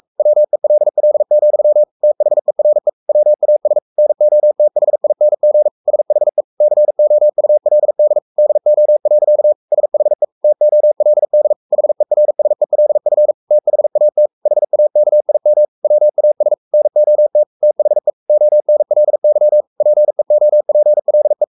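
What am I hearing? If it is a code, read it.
Morse code, heard as a keyed tone, about 44 wpm, 600 Hz. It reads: WELL, THERE WAS NOTHING SHE COULD DO. SHE TOLD HERSELF THAT HAKIM WAS NOT THE ONLY PEOPLE
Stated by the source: Morse code at 43 wpm sending Ken Follett's novel Never in English, from CW med Gnister.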